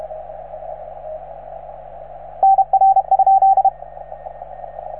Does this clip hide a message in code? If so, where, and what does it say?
Yes: Morse code from 2.4 to 3.7 s, heard as a keyed tone, about 34 wpm, 770 Hz, with radio noise. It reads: NR?